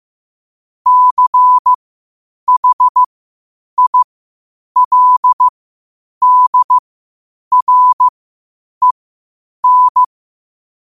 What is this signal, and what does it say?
Morse code, heard as a keyed tone, about 15 wpm, spaced as for 8 wpm, 1000 Hz. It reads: CHILDREN
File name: qotdfiles / 8wpm